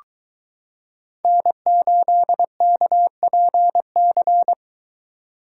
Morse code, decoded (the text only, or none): N8KPC